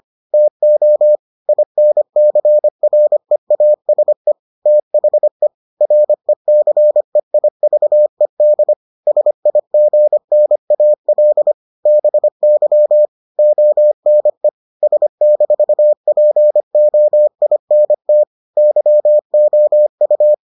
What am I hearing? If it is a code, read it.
Morse code, heard as a keyed tone, about 25 wpm, 600 Hz. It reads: TO INCREASE THE RECEIVED SIGNAL BY ONE S-POINT YOU